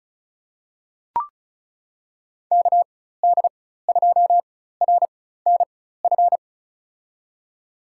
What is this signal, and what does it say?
Morse code, heard as a keyed tone, about 34 wpm, spaced as for 16 wpm, 690 Hz. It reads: KD2RNF
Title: USA Call Signs - 4x Character Spacing - 35wpm